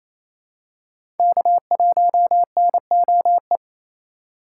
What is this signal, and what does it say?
Morse code, heard as a keyed tone, about 28 wpm, 700 Hz. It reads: K1NOE
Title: USA Call Signs - 5 Characters - 2x Character Spacing - 28wpm